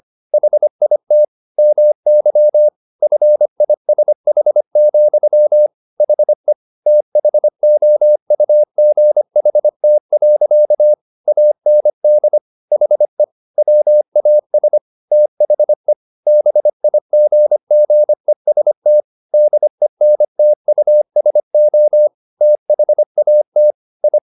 Morse code, heard as a keyed tone, about 25 wpm, 600 Hz. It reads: HIT MY FISH, HE THOUGHT. AND HE WAS THE BIGGEST DENTUSO THAT I